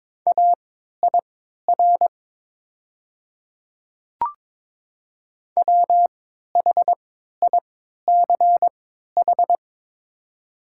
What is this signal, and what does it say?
Morse code, heard as a keyed tone, about 22 wpm, spaced as for 12 wpm, 700 Hz. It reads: AIR WHICH
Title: 3x Character Spacing - Top 200 Words - 22wpm